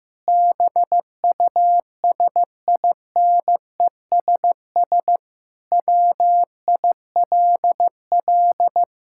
Morse code, heard as a keyed tone, about 15 wpm, 700 Hz. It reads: BUSINESS WILL
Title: Sentences from Top 1000 Words - Review Part 5 - 15wpm